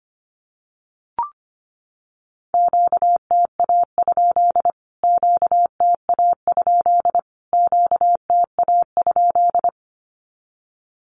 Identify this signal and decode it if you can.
Morse code, heard as a keyed tone, about 25 wpm, 700 Hz. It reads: QTA? QTA? QTA?
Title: Q-Codes Repeated 3x 25wpm